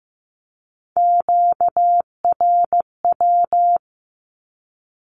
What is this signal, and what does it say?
Morse code, heard as a keyed tone, about 15 wpm, 700 Hz. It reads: QRW
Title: Q-Codes 15wpm